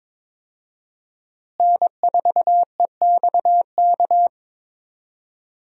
Morse code, heard as a keyed tone, about 22 wpm, 700 Hz. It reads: N4EXK
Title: USA Call Signs - 5 Characters - Repeated 2x - 22wpm